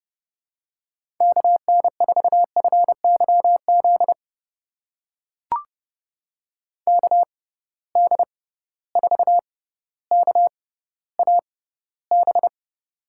Morse code, heard as a keyed tone, about 32 wpm, spaced as for 16 wpm, 700 Hz. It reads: KN4FYZ KD4KAB
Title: USA Call Signs - 6 Characters - 6x Character Spacing - 30wpm